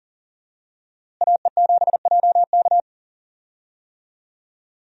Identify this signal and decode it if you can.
Morse code, heard as a keyed tone, about 40 wpm, 700 Hz. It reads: AE7JK
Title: USA Call Signs - 5 Characters - Repeated 2x - 40wpm